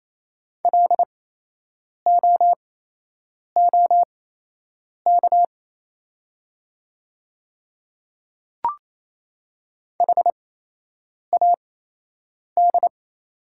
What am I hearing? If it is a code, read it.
Morse code, heard as a keyed tone, about 28 wpm, spaced as for 8 wpm, 700 Hz. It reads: LOOK HAD
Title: 8x Character Spacing - Top 100 Words - 28wpm